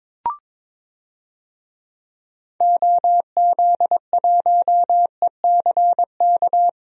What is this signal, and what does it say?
Morse code, heard as a keyed tone, about 22 wpm, 700 Hz. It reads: OZ1ECK
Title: DXCC-100 Call Signs - 6 Characters - Repeated 3x - 22wpm